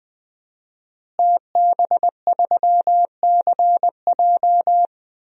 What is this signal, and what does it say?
Morse code, heard as a keyed tone, about 20 wpm, 700 Hz. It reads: TB3CJ
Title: DXCC-100 Call Signs - 5 Characters - Repeated 2x - 20wpm